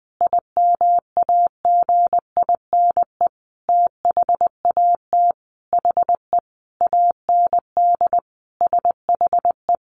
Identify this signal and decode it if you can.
Morse code, heard as a keyed tone, about 20 wpm, 700 Hz. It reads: IMAGINE THAT HE AND SHE
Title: Sentences from Top 1400 Words 20wpm